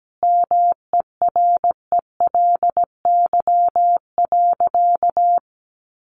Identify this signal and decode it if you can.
Morse code, heard as a keyed tone, about 17 wpm, 700 Hz. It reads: MERELY.
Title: Sentences from Top 1400 Words 17wpm